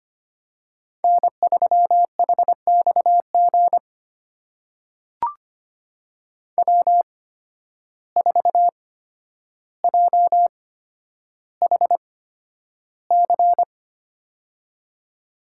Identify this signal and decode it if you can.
Morse code, heard as a keyed tone, about 25 wpm, spaced as for 11 wpm, 700 Hz. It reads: N3HXG W4JHC
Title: USA Call Signs - 5 Characters - 8x Character Spacing - 25wpm